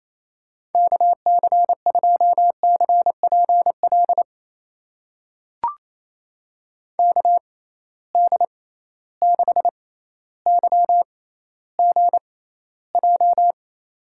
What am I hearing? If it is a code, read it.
Morse code, heard as a keyed tone, about 28 wpm, spaced as for 14 wpm, 700 Hz. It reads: KC2CPL KD6YGJ
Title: USA Call Signs - 6 Characters - 6x Character Spacing - 28wpm